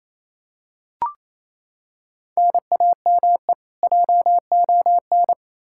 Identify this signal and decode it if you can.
Morse code, heard as a keyed tone, about 28 wpm, 700 Hz. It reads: NAME JON
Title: K1USN SST QSO Practice 28wpm